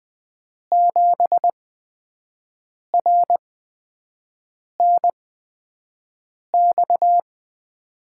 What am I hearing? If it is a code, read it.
Morse code, heard as a keyed tone, about 20 wpm, spaced as for 5 wpm, 700 Hz. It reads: 7RNX
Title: USA Call Signs - 5 Characters - 8x Character Spacing - 20wpm